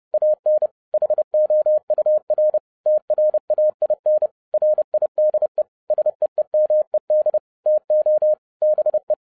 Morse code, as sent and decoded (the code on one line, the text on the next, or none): .- -. / .... --- ..- .-. / - .-. .- .. -. / .-. .. -.. / ... . -- . -.. / - --- / -... .
AN HOUR TRAIN RIDE SEEMED TO BE